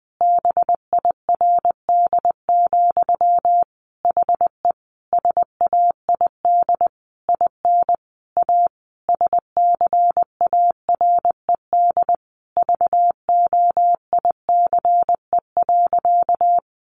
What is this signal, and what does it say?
Morse code, heard as a keyed tone, about 20 wpm, 700 Hz. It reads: BIRD, HE SAID IN A SCARED VOICE.
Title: Sentences from Top 700 Words 20wpm